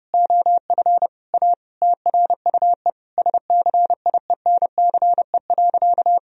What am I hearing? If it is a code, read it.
Morse code, heard as a keyed tone, about 30 wpm, 700 Hz. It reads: OF A TRUE SCIENCE.
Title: Sentences from Top 1000 Words - Review Part 4 - 30wpm